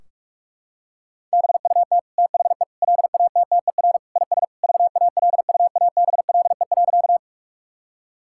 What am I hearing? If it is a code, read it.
Morse code, heard as a keyed tone, about 46 wpm, 700 Hz. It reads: BUT THE LATTER IS VALUABLE.
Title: Sentences from Top 1600 Words 45wpm